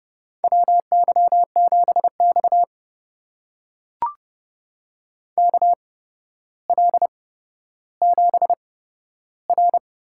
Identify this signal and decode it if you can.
Morse code, heard as a keyed tone, about 30 wpm, spaced as for 13 wpm, 700 Hz. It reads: WY7X KL7R
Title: USA Call Signs - 4 Characters - 8x Character Spacing - 30wpm